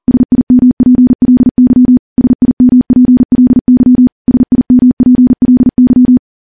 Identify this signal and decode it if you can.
Morse code, heard as a keyed tone, about 40 wpm, 250 Hz. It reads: SIMPLY SIMPLY SIMPLY
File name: morse / morse-training-40WPM